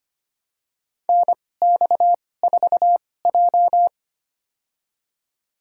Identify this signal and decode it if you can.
Morse code, heard as a keyed tone, about 25 wpm, spaced as for 18 wpm, 700 Hz. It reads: NX4J
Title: USA Call Signs - 4 Characters - 2x Character Spacing - 25wpm